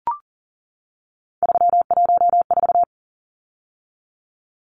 Morse code, heard as a keyed tone, about 40 wpm, 700 Hz. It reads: 314
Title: Numbers - 3-Digit 40wpm